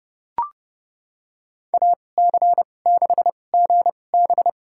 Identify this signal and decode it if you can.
Morse code, heard as a keyed tone, about 30 wpm, spaced as for 22 wpm, 700 Hz. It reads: AC6GB
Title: USA Call Signs - 5 Characters - 2x Character Spacing - 30wpm